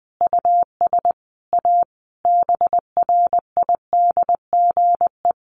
Morse code, heard as a keyed tone, about 20 wpm, 700 Hz. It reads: US A BRIDGE
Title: Sentences from Top 1200 Words 20wpm